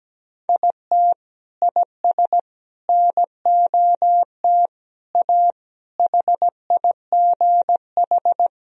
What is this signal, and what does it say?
Morse code, heard as a keyed tone, about 17 wpm, 700 Hz. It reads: IT IS NOT A HIGH
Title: Sentences from Top 1000 Words - Review Part 4 - 17wpm